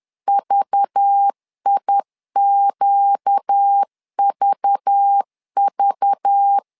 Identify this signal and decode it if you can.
Morse code, heard as a keyed tone, about 10 wpm, 800 Hz. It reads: VIQVV
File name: LEZIONI / 5 LBVC / 5 LBV 10wpm 2248